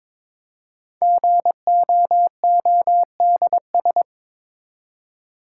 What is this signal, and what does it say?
Morse code, heard as a keyed tone, about 22 wpm, 700 Hz. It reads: GOODS
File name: 5 Letter Words - First 3 Letters Given - 22wpm